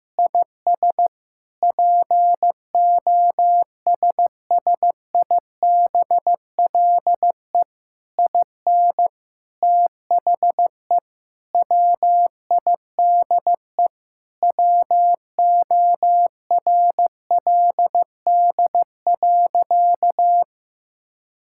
Morse code, heard as a keyed tone, about 15 wpm, 700 Hz. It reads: IS POSSIBLE IN THE WIDE WORLD.
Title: Sentences from Top 1000 Words - Review Part 6 - 15wpm